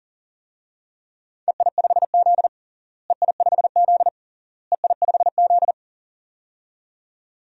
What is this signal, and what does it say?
Morse code, heard as a keyed tone, about 40 wpm, 700 Hz. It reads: EI5Z EI5Z EI5Z
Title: DXCC-100 Call Signs - 4 Characters - Repeated 3x - 40wpm